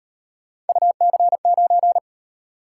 Morse code, heard as a keyed tone, about 36 wpm, 700 Hz. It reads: UC9